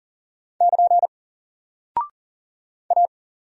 Morse code, heard as a keyed tone, about 38 wpm, 700 Hz. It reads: <KN> A
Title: Single Character (Letter:Number:Punctuation:Prosign) Alt 40wpm